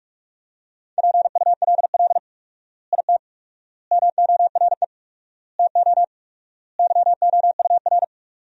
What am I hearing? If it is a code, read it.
Morse code, heard as a keyed tone, about 41 wpm, 700 Hz. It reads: PULL IT MORE TO YOUR
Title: 4x Word-Spacing - Sentences from Top 400 Words - 45wpm